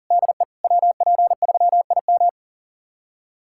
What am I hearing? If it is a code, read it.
Morse code, heard as a keyed tone, about 37 wpm, 700 Hz. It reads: DE WP3IM
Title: USA Callsign DE Callsign - 5 Characters - 40wpm